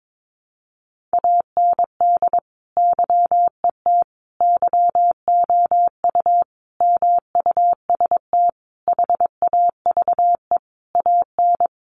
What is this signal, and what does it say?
Morse code, heard as a keyed tone, about 22 wpm, 700 Hz. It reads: AND YET YOU MUST HAVE AN
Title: Sentences from Top 1300 Words 22wpm